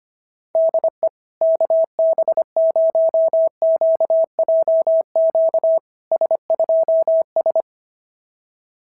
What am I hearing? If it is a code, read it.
Morse code, heard as a keyed tone, about 25 wpm, 660 Hz. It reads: DE KB0QJQ S2S
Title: SOTA QSO Practice 25wpm